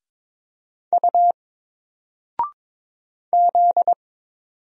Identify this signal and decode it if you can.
Morse code, heard as a keyed tone, about 22 wpm, 700 Hz. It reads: U Z
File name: Single-Letter-Number-22wpm